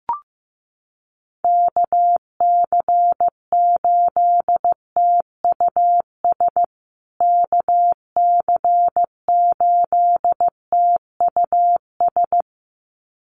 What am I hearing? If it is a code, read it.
Morse code, heard as a keyed tone, about 15 wpm, 700 Hz. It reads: KC8TUS KC8TUS
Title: USA Call Signs Repeated 2x 15wpm